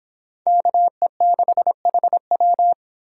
Morse code, heard as a keyed tone, about 31 wpm, 700 Hz. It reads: KE6HW